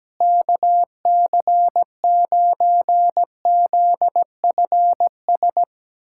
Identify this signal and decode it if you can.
Morse code, heard as a keyed tone, about 17 wpm, 700 Hz. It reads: KC9ZFS
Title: USA Callsign DE Callsign - 6 Characters - 17wpm